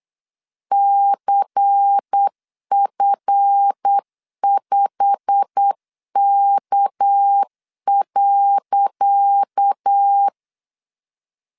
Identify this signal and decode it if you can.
Morse code, heard as a keyed tone, about 8 wpm, 800 Hz. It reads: CF5K.
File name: LEZIONI / 10 SIGNS / 10C DOT 8wpm 3649